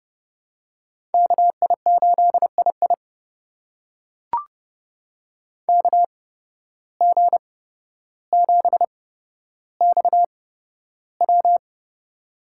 Morse code, heard as a keyed tone, about 30 wpm, spaced as for 13 wpm, 700 Hz. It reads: KI8II KG7XW
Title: USA Call Signs - 5 Characters - 8x Character Spacing - 30wpm